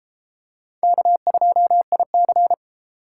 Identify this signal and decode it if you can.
Morse code, heard as a keyed tone, about 32 wpm, 700 Hz. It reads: K2IC